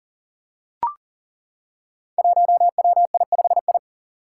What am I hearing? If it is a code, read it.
Morse code, heard as a keyed tone, about 40 wpm, 700 Hz. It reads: 1WI5I
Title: Five Challenging Letter-Number Combinations 40wpm